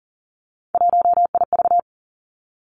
Morse code, heard as a keyed tone, about 39 wpm, 700 Hz. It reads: 1IV
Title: Three Challenging Letter-Number Combinations 40wpm